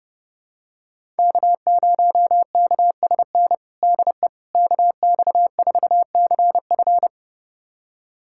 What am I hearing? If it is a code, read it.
Morse code, heard as a keyed tone, about 32 wpm, 700 Hz. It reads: K0KSN DE KX4CF
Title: USA Callsign DE Callsign 30wpm